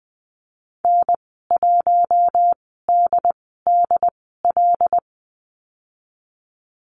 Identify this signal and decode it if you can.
Morse code, heard as a keyed tone, about 20 wpm, spaced as for 15 wpm, 700 Hz. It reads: N1DDL
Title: 088 Lesson 15g - Number 5 - Call Signs 20wpm